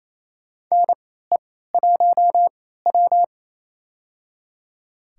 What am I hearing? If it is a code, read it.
Morse code, heard as a keyed tone, about 28 wpm, spaced as for 16 wpm, 700 Hz. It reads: NE1W
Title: USA Call Signs - 4 Characters - 3x Character Spacing - 28wpm